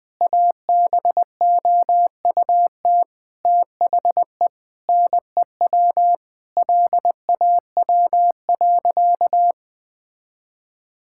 Morse code, heard as a keyed tone, about 21 wpm, 700 Hz. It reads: ABOUT THE NEW LAW.
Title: Sentences with 8-Word Phrases - Spoken Beginning - 20wpm